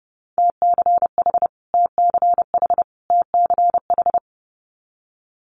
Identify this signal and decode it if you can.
Morse code, heard as a keyed tone, about 30 wpm, 700 Hz. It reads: TCH TCH TCH
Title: Multi-letter Phonemes Repeated 3x 30wpm